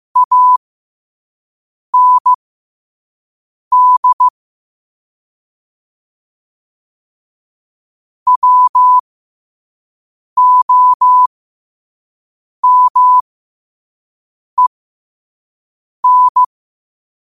Morse code, heard as a keyed tone, about 15 wpm, spaced as for 5 wpm, 1000 Hz. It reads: AND WOMEN
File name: qotdfiles / 5wpm